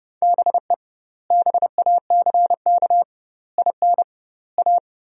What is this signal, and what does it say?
Morse code, heard as a keyed tone, about 30 wpm, 700 Hz. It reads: BE BACK IN A
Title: 2x Word-Spacing - Sentences from Top 500 Words - 30wpm